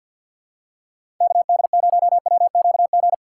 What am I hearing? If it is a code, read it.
Morse code, heard as a keyed tone, about 53 wpm, 700 Hz. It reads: KD0WQG
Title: USA Call Signs - 6x Character Spacing - 50wpm